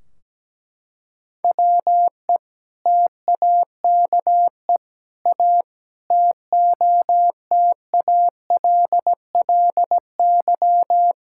Morse code, heard as a keyed tone, about 17 wpm, 700 Hz. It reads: WE TAKE A TOTALLY
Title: Sentences from Top 1600 Words 17wpm